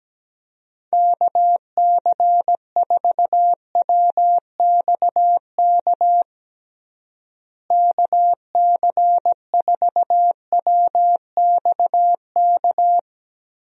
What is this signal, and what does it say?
Morse code, heard as a keyed tone, about 17 wpm, 700 Hz. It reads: KC4WXK KC4WXK
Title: USA Call Signs - 6 Characters - Repeated 3x - 17wpm